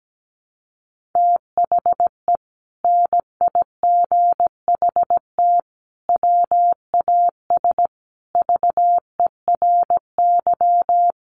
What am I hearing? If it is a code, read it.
Morse code, heard as a keyed tone, about 17 wpm, 700 Hz. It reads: THE NIGHT WAS VERY
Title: Sentences from Top 500 Words 17wpm